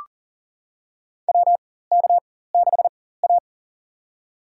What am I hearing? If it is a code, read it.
Morse code, heard as a keyed tone, about 40 wpm, spaced as for 19 wpm, 700 Hz. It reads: WK6A
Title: USA Call Signs - 4x Character Spacing - 40wpm